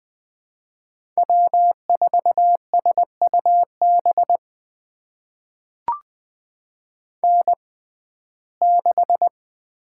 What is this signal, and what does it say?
Morse code, heard as a keyed tone, about 20 wpm, spaced as for 15 wpm, 700 Hz. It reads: W4SUB N6
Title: USA Call Signs - 6x Character Spacing - 20wpm